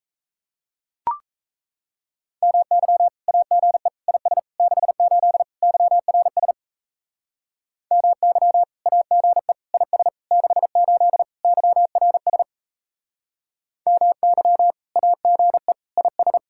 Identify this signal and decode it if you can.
Morse code, heard as a keyed tone, about 35 wpm, 700 Hz. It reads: MY AGE IS 68 YRS MY AGE IS 68 YRS MY AGE IS